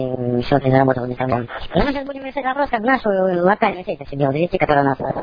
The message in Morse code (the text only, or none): none